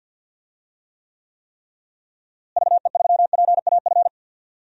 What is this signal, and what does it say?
Morse code, heard as a keyed tone, about 47 wpm, 700 Hz. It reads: VE3PAF
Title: DXCC-100 Call Signs - 6 Characters - Repeated 3x - 50wpm